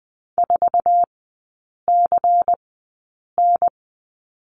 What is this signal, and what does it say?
Morse code, heard as a keyed tone, about 20 wpm, spaced as for 8 wpm, 700 Hz. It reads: 4CN
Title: 093 Lesson 16e - Letter c - Three Characters 20wpm